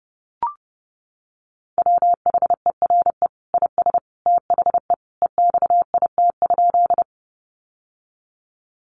Morse code, heard as a keyed tone, about 30 wpm, 700 Hz. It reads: WHERE IS THE EXIT?
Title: Sentences from Top 1100 Words 30wpm